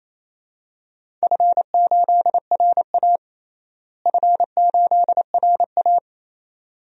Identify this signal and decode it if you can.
Morse code, heard as a keyed tone, about 28 wpm, 700 Hz. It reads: F8RA F8RA
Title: DXCC-100 Call Signs - 4 Characters - Repeated 2x - 28wpm